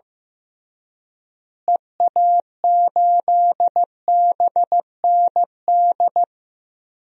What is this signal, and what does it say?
Morse code, heard as a keyed tone, about 15 wpm, 700 Hz. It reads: EA8BND
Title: DXCC-100 Call Signs - 6 Characters - Repeated 3x - 15wpm